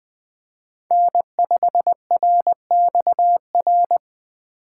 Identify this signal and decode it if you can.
Morse code, heard as a keyed tone, about 20 wpm, 700 Hz. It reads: N5RXR
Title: USA Call Signs - 6x Character Spacing - 20wpm